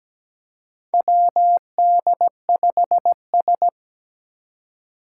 Morse code, heard as a keyed tone, about 17 wpm, 700 Hz. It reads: WD5S
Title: Four Challenging Letter-Number Combinations 17wpm